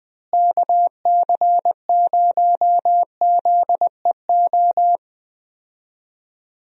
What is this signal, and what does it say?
Morse code, heard as a keyed tone, about 20 wpm, 700 Hz. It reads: KC0ZEO
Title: USA Call Signs - 6 Characters - Repeated 2x - 20wpm